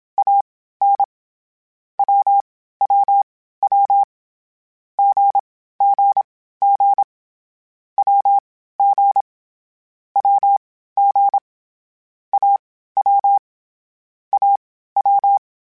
Morse code, heard as a keyed tone, about 27 wpm, spaced as for 15 wpm, 800 Hz. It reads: AN WWW GGG WG WG AW AW